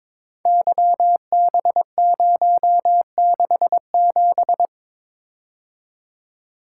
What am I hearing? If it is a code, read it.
Morse code, heard as a keyed tone, about 23 wpm, 690 Hz. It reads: YB067
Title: Code Groups - 5 Letter:Numbers - 22wpm